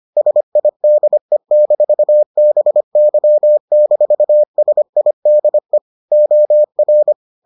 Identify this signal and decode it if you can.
Morse code, heard as a keyed tone, about 25 wpm, 600 Hz. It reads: SIDE-BY-SIDE OR